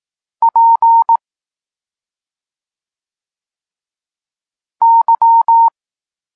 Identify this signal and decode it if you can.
Morse code, heard as a keyed tone, about 18 wpm, 920 Hz. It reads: P Y
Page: Tekst 32 – Doskonalenie odbioru poznanych liter